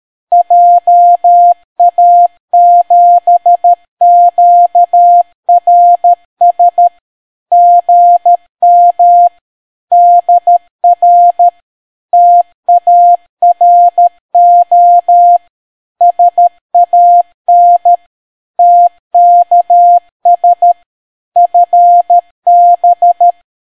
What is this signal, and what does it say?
Morse code, heard as a keyed tone, about 13 wpm, 700 Hz. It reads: JA7QRS GM DR TARO SAN TKS FB